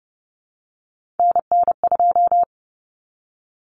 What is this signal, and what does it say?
Morse code, heard as a keyed tone, about 30 wpm, 700 Hz. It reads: NN2
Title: Common Cut Numbers - 3-Digit 30wpm